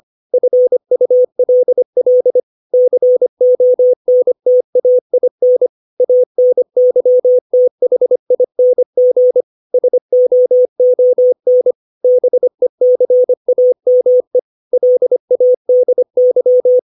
Morse code, heard as a keyed tone, about 25 wpm, 500 Hz. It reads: FULL CONTAIN ANYTHING SOON BECAME LADY
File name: 632 words / 632 Words 25WPM